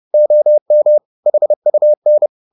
Morse code, heard as a keyed tone, about 30 wpm, 600 Hz. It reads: OM HUN
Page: Ferie 30 wpm